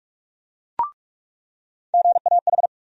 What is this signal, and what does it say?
Morse code, heard as a keyed tone, about 41 wpm, 700 Hz. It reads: GAH
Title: USA Call Sign Postfixes - 3 Letters 45wpm